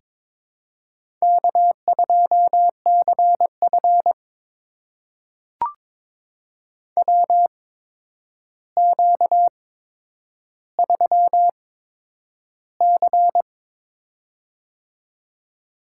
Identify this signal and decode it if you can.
Morse code, heard as a keyed tone, about 22 wpm, spaced as for 9 wpm, 700 Hz. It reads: K2CF WQ3C
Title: USA Call Signs - 4 Characters - 8x Character Spacing - 22wpm